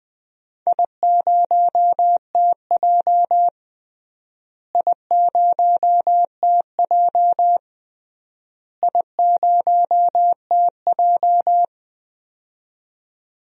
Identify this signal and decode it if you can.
Morse code, heard as a keyed tone, about 20 wpm, 700 Hz. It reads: I0TJ I0TJ I0TJ
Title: DXCC-100 Call Signs - 4 Characters - Repeated 3x - 20wpm